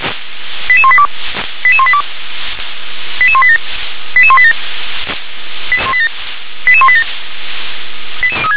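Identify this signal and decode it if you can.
Five-tone selective call (ZVEI), heard as quick runs of five tones, tones of about 70 ms. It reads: 88182 88182 88187 88187 88187
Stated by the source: ZVEI five-tone call received by SDR.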